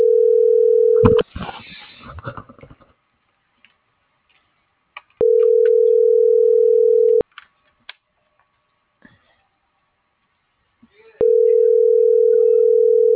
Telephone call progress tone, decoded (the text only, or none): ringback tone